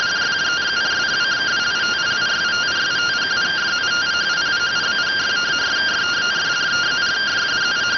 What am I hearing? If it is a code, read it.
Two-tone teleprinter signal, about 170 Hz shift.